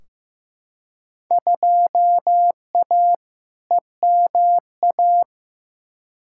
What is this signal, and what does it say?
Morse code, heard as a keyed tone, about 15 wpm, 700 Hz. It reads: 2A EMA
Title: Field Day Exchanges - 15wpm